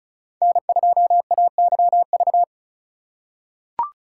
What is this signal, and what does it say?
Morse code, heard as a keyed tone, about 36 wpm, 700 Hz. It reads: N2AYV
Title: USA Call Signs - 4x Character Spacing - 35wpm